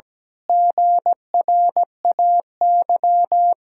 Morse code, heard as a keyed tone, about 17 wpm, 700 Hz. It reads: GRAY
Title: Pangram Challenge v2 - 17wpm